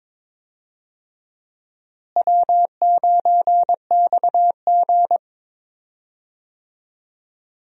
Morse code, heard as a keyed tone, about 22 wpm, 700 Hz. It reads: W9XG
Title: USA Call Signs - 4 Characters - Repeated 2x - 22wpm